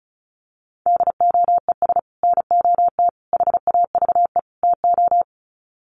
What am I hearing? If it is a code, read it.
Morse code, heard as a keyed tone, about 34 wpm, 700 Hz. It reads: DOES NOT HAVE TO
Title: Sets of 4 Words 35wpm